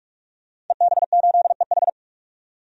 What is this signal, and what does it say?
Morse code, heard as a keyed tone, about 42 wpm, 700 Hz. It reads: EB8EH
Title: DXCC-100 Call Signs - 5 Characters - 45wpm